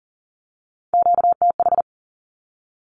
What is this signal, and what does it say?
Morse code, heard as a keyed tone, about 40 wpm, 700 Hz. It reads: QTH
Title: Q-Codes with Spoken Meaning - 40wpm